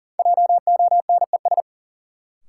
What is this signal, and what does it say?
Morse code, heard as a keyed tone, about 40 wpm, 700 Hz. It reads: JONES
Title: Sets of 5 Words - New Word Drill - 40wpm